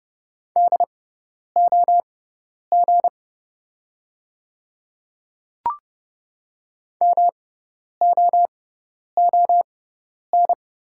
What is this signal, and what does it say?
Morse code, heard as a keyed tone, about 30 wpm, spaced as for 10 wpm, 700 Hz. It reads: DOG MOON
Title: 6x Character Spacing - Top 500 Words - 30wpm